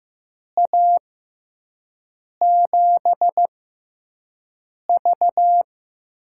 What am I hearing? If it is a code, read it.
Morse code, heard as a keyed tone, about 16 wpm, spaced as for 5 wpm, 700 Hz. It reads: A7V
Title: USA Call Signs - 6x Character Spacing - 15wpm